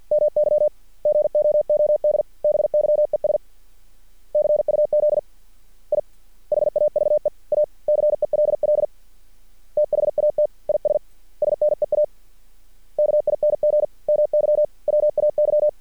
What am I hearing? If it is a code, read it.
Morse code, heard as a keyed tone, about 50 wpm, spaced as for 41 wpm, 600 Hz. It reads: MY GOODBYES CUZ I HAVE A CELL THAT IS SNEAKING MY WAY
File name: archived / CRSnet